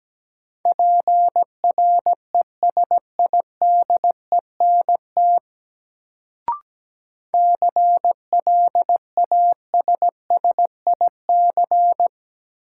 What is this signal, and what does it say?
Morse code, heard as a keyed tone, about 17 wpm, 700 Hz. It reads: PRESIDENT CLASSIC
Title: Sentences from Top 1000 Words - New Word Drill - 17wpm